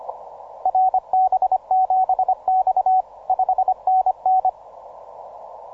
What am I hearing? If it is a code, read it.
Morse code, heard as a keyed tone, about 25 wpm, 750 Hz. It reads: RB7X 5NN
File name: simple-file-list / JW LB1QI 17MCW